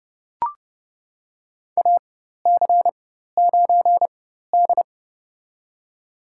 Morse code, heard as a keyed tone, about 31 wpm, spaced as for 14 wpm, 700 Hz. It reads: AC9D
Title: USA Call Signs - 4x Character Spacing - 30wpm